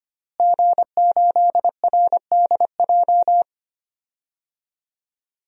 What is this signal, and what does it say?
Morse code, heard as a keyed tone, about 25 wpm, 700 Hz. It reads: G8RDJ
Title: DXCC-100 Call Signs - 5 Characters - Repeated 2x - 25wpm